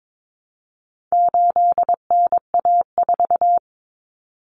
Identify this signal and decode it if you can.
Morse code, heard as a keyed tone, about 22 wpm, 700 Hz. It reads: 8NA4
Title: Common Cut Numbers - 4-Digit 22wpm